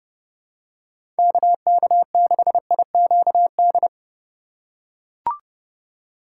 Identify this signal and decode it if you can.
Morse code, heard as a keyed tone, about 29 wpm, 700 Hz. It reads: KK6IQD